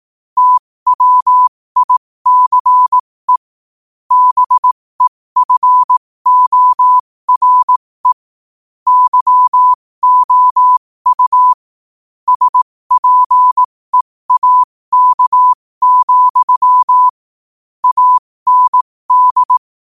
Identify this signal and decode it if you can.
Morse code, heard as a keyed tone, about 18 wpm, spaced as for 15 wpm, 1000 Hz. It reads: TWICE BEFORE YOU SPEAK, AND
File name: qotdfiles / 15wpm